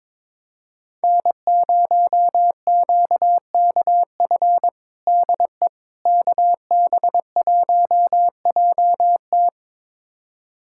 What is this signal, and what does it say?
Morse code, heard as a keyed tone, about 22 wpm, 700 Hz. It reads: N0QKF DE KB1JT
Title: USA Callsign DE Callsign - 5 Characters - 22wpm